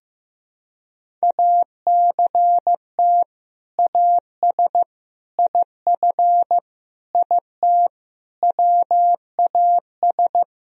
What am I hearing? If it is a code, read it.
Morse code, heard as a keyed tone, about 15 wpm, 700 Hz. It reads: ACT AS IF IT WAS
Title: Sentences from Top 1000 Words - Review Part 1 - 15wpm